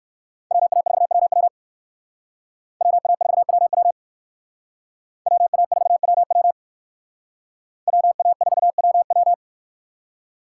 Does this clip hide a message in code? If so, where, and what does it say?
Yes: Morse code from 0.5 to 9.3 s, heard as a keyed tone, about 62 wpm, 690 Hz. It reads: WA4WW WA4WW WA4WW WA4WW